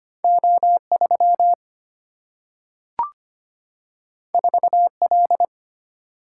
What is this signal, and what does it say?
Morse code, heard as a keyed tone, about 25 wpm, 700 Hz. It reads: O 3 4 L
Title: Single-Letter Single-Number 25wpm